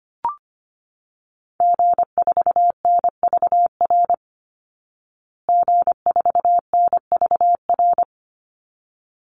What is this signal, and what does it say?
Morse code, heard as a keyed tone, about 25 wpm, 700 Hz. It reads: G4NVR G4NVR
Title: Call Signs Repeated 3x - 25wpm